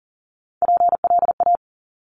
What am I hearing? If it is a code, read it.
Morse code, heard as a keyed tone, about 38 wpm, 700 Hz. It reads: PLA
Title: Top Three Letter Combinations in Words 40wpm